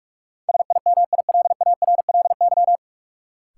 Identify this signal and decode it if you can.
Morse code, heard as a keyed tone, about 41 wpm, 700 Hz. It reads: SIMILARLY
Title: Sentences from Top 1400 Words - New Word Drill - 45wpm